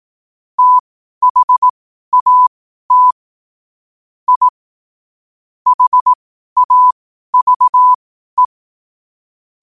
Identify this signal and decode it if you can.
Morse code, heard as a keyed tone, about 18 wpm, spaced as for 12 wpm, 1000 Hz. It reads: THAT I HAVE
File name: qotdfiles / 12wpm